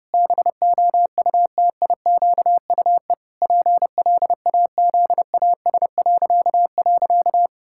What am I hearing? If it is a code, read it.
Morse code, heard as a keyed tone, about 32 wpm, 700 Hz. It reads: BOUTIQUE PLAZAS..
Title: Pangram Challenge v2 - 30wpm